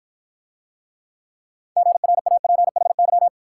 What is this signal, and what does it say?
Morse code, heard as a keyed tone, about 54 wpm, 700 Hz. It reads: GRAPHY